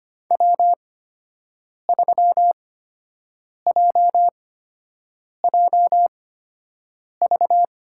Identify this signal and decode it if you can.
Morse code, heard as a keyed tone, about 25 wpm, spaced as for 7 wpm, 700 Hz. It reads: W3JJV
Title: USA Call Signs - 5 Characters - 8x Character Spacing - 25wpm